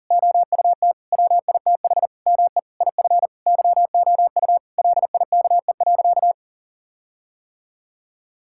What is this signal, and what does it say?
Morse code, heard as a keyed tone, about 40 wpm, 700 Hz. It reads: OUT WITH ME IF YOU LIKE.
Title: Sentences from Top 1600 Words 40wpm